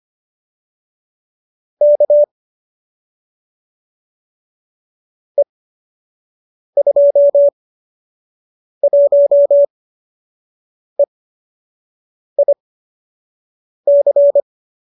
Morse code, heard as a keyed tone, about 25 wpm, spaced as for 6 wpm, 570 Hz. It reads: K E21EIC